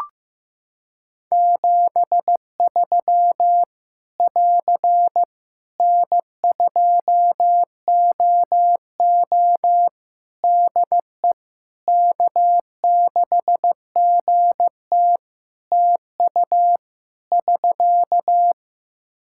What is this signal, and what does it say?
Morse code, heard as a keyed tone, about 15 wpm, 700 Hz. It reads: 73 <AR> N2OO DE K6GT TU <SK>
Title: Practice QSO - Exchange 4 - Ending - 15wpm